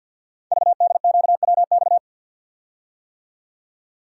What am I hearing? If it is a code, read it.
Morse code, heard as a keyed tone, about 53 wpm, 700 Hz. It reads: VDQWX
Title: Code Groups - 5 Letter:Numbers - 50wpm